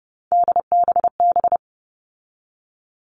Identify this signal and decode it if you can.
Morse code, heard as a keyed tone, about 30 wpm, 700 Hz. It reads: DBB
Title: All Cut Numbers - 3-Digit 30wpm